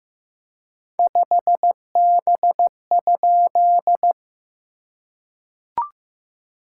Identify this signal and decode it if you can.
Morse code, heard as a keyed tone, about 15 wpm, 690 Hz. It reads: 5B?
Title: SOTA QSO Practice 15wpm